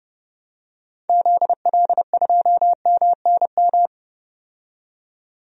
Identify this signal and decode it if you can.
Morse code, heard as a keyed tone, about 30 wpm, 700 Hz. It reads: ZL2MNM
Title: DXCC-100 Call Signs - 6 Characters - 30wpm